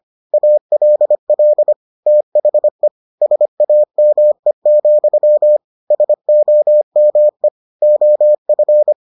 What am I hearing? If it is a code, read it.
Morse code, heard as a keyed tone, about 25 wpm, 600 Hz. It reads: ALL THE SAME, SOME OF